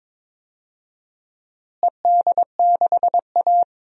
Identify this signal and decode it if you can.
Morse code, heard as a keyed tone, about 21 wpm, 700 Hz. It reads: ED6A
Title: DXCC-100 Call Signs - 4 Characters - Repeated 3x - 22wpm